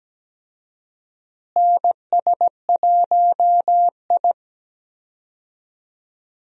Morse code, heard as a keyed tone, about 17 wpm, 700 Hz. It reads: NS1I
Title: USA Call Signs - 4 Characters - Repeated 3x - 17wpm